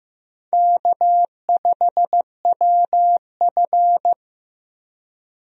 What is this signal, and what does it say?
Morse code, heard as a keyed tone, about 15 wpm, 700 Hz. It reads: K5WF
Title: USA Call Signs - 4 Characters - Repeated 3x - 15wpm